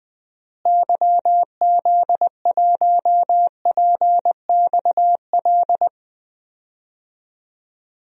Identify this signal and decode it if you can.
Morse code, heard as a keyed tone, about 21 wpm, 700 Hz. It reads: YZ1PXL
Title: DXCC-100 Call Signs - 6 Characters - Repeated 3x - 20wpm